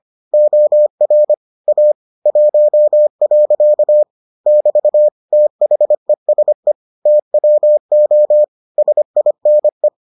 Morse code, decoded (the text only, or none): OR A 1. <BT> THESE TWO SINE